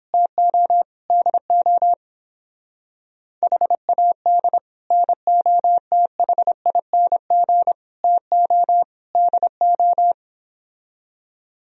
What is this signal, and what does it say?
Morse code, heard as a keyed tone, about 26 wpm, 700 Hz. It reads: TO DO HAD NOTHING TO DO
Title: Speed-Racing - Sets of 4 Words 22wpm